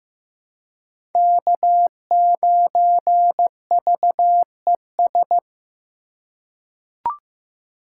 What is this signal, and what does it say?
Morse code, heard as a keyed tone, about 15 wpm, 700 Hz. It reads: K9VES
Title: USA Call Signs - 5 Characters - 6x Character Spacing - 15wpm